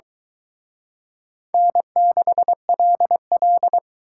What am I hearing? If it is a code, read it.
Morse code, heard as a keyed tone, about 24 wpm, 700 Hz. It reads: N6LL